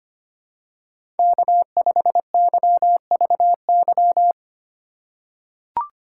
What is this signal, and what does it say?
Morse code, heard as a keyed tone, about 25 wpm, 700 Hz. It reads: K5YVY
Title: K1USN SST QSO Practice - 25wpm